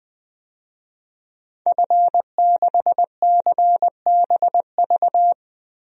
Morse code, heard as a keyed tone, about 20 wpm, 700 Hz. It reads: F6CBV
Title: DXCC-100 Call Signs - 5 Characters - 20wpm